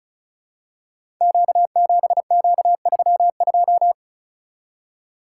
Morse code, heard as a keyed tone, about 36 wpm, 700 Hz. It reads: Q7Q32
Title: Code Groups - 5 Letter:Numbers - 35wpm